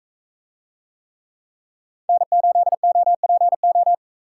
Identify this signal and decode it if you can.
Morse code, heard as a keyed tone, about 41 wpm, 700 Hz. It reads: N8OPO